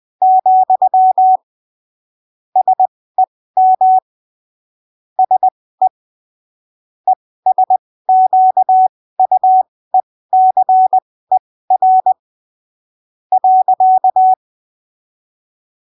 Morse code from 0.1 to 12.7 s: , SEM SE ESQUECER